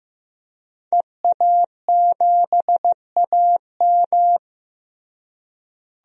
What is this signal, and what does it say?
Morse code, heard as a keyed tone, about 15 wpm, 700 Hz. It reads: EA7AM
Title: DXCC-100 Call Signs - 5 Characters - Repeated 2x - 15wpm